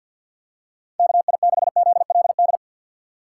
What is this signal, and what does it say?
Morse code, heard as a keyed tone, about 53 wpm, 700 Hz. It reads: KI6ZLD
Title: USA Call Signs - 6 Characters - 8x Character Spacing - 50wpm